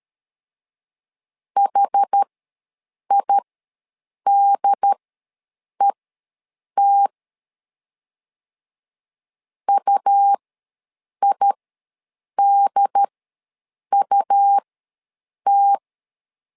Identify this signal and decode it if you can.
Morse code, heard as a keyed tone, about 13 wpm, spaced as for 7 wpm, 800 Hz. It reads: HIDET UIDUT
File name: LEZIONI / 2 UHDN / 2 UHD ws 12wpm 1722